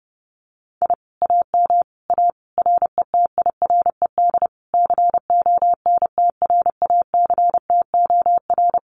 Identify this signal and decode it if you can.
Morse code, heard as a keyed tone, about 30 wpm, 700 Hz. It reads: I AM A RETIRED CONTRACTOR
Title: Common QSO Phrases Encore 30wpm